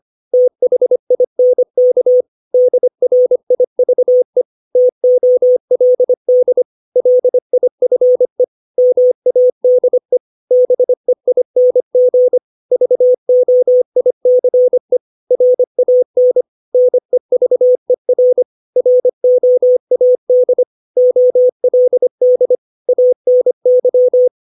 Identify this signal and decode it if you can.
Morse code, heard as a keyed tone, about 25 wpm, 500 Hz. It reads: THINK DRIVE TOLD LIFE MADE BEING VOICE RAN NEVER ROAD OLD ANY
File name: 632 words / 632 Words 25WPM